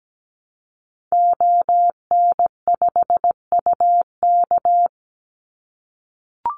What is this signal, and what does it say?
Morse code, heard as a keyed tone, about 17 wpm, 700 Hz. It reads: ON5UK
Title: Call Signs - 17wpm